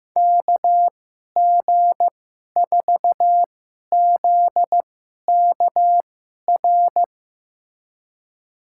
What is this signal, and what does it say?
Morse code, heard as a keyed tone, about 16 wpm, spaced as for 11 wpm, 700 Hz. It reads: KG4ZKR
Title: USA Call Signs - 6 Characters - 2x Character Spacing - 15wpm